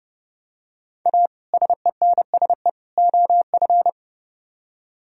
Morse code, heard as a keyed tone, about 28 wpm, 700 Hz. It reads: A SENSE OF